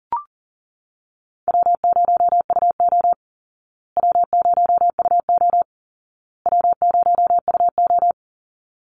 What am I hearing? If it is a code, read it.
Morse code, heard as a keyed tone, about 40 wpm, 700 Hz. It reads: W0UO W0UO W0UO
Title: Call Signs Repeated 2x - 40wpm